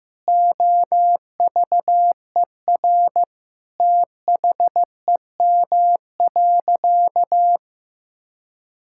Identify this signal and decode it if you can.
Morse code, heard as a keyed tone, about 15 wpm, 700 Hz. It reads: OVER THEM.
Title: Sentences from Top 1000 Words - Review Part 6 - 15wpm